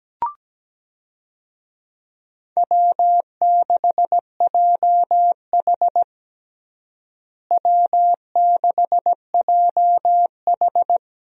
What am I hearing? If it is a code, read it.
Morse code, heard as a keyed tone, about 17 wpm, 700 Hz. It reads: W6JH W6JH
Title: USA Call Signs - 4 Characters - Repeated 3x - 17wpm